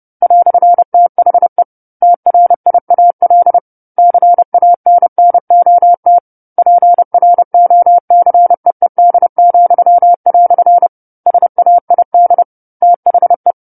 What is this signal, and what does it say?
Morse code, heard as a keyed tone, about 30 wpm, 700 Hz. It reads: "THE TRIAL CANNOT PROCEED," SAID THE
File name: 10-Chapter-10 30 WPM